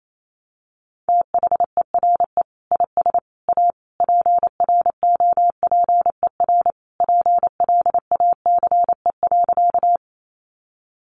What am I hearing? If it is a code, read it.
Morse code, heard as a keyed tone, about 28 wpm, 700 Hz. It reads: THERE IS A PROPER PLACE.
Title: Sentences from Top 800 Words 28wpm